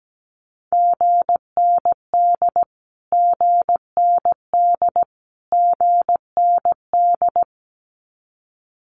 Morse code, heard as a keyed tone, about 17 wpm, 700 Hz. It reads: GND GND GND
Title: Abbreviations Repeated 3x - 17wpm